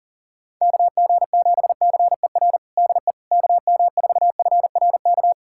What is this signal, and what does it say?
Morse code, heard as a keyed tone, about 40 wpm, 700 Hz. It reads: KG7CER DE KM4FRK
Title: USA Callsign DE Callsign 40wpm